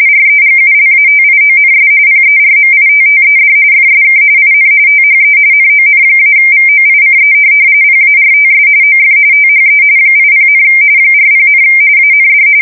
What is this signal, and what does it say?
RTTY, about 45.45 baud. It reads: REGULATIONS PERMITTED FSK IN THE NON-VOICE PARTS OF THE 80, 40, AND
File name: fsk170 / teletype-RTTY